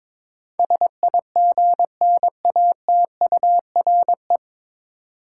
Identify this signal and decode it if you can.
Morse code, heard as a keyed tone, about 22 wpm, 700 Hz. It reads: SIGNATURE
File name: 9 Letter Words - First 3 Letters Given - 22wpm